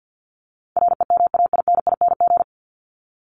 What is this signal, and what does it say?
Morse code, heard as a keyed tone, about 49 wpm, 700 Hz. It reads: REMAINING
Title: ING Suffix - Top 100 Words with Base Words 50wpm